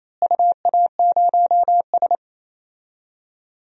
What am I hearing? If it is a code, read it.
Morse code, heard as a keyed tone, about 28 wpm, 700 Hz. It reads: UA0S
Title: DXCC-100 Call Signs - 4 Characters - Repeated 3x - 28wpm